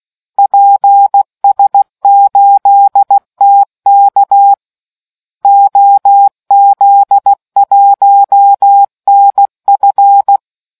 Morse code, heard as a keyed tone, about 16 wpm, 800 Hz. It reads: PS8TK OZ1NF